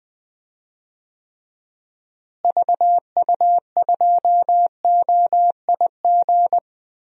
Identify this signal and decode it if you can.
Morse code, heard as a keyed tone, about 20 wpm, 700 Hz. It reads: VU2OIG
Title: DXCC-100 Call Signs - 6 Characters - Repeated 2x - 20wpm